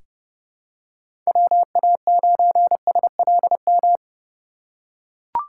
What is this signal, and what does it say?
Morse code, heard as a keyed tone, about 30 wpm, 700 Hz. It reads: WA9SLM
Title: USA Call Signs - 6 Characters - 3x Character Spacing - 30wpm